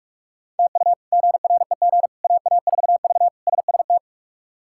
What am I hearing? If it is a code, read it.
Morse code, heard as a keyed tone, about 42 wpm, 700 Hz. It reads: TU GREG AA4V SST
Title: K1USN SST QSO Practice - 45wpm